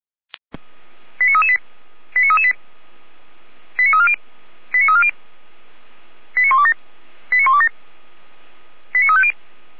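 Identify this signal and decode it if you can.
Five-tone selective call (ZVEI), heard as quick runs of five tones, tones of about 70 ms. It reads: 89338 89338 89340 89340 89127 89127 89350
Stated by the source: Five-tone ZVEI selective call 89330 received on an SDR.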